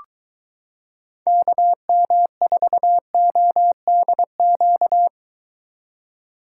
Morse code, heard as a keyed tone, about 23 wpm, 700 Hz. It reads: KM4ODQ